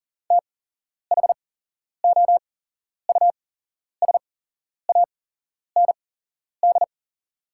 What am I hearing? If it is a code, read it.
Morse code, heard as a keyed tone, about 40 wpm, spaced as for 11 wpm, 690 Hz. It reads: THOUSAND
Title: 8x Character Spacing - Top 500 Words - 40wpm